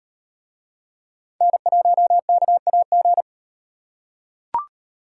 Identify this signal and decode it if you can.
Morse code, heard as a keyed tone, about 36 wpm, 700 Hz. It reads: N1KAG